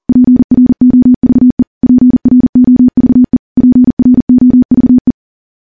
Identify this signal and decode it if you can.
Morse code, heard as a keyed tone, about 40 wpm, 250 Hz. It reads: PROVE PROVE PROVE
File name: morse / morse-training-40WPM